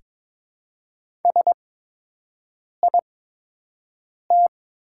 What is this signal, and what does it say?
Morse code, heard as a keyed tone, about 22 wpm, spaced as for 6 wpm, 700 Hz. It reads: SIT
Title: 8x Character Spacing - Top 500 Words - 22wpm